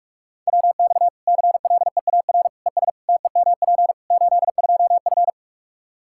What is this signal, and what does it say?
Morse code, heard as a keyed tone, about 41 wpm, 700 Hz. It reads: WX CLEAR ES TEMP 82F
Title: Practice QSO - Exchange 2 - Info - 45wpm